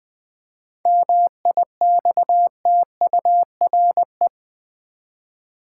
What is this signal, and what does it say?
Morse code, heard as a keyed tone, about 20 wpm, 700 Hz. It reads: MIXTURE
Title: Seven Letter Words - 20wpm